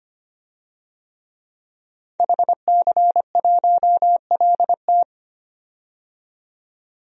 Morse code, heard as a keyed tone, about 25 wpm, 700 Hz. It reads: HC1LT
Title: DXCC-100 Call Signs - 5 Characters - 25wpm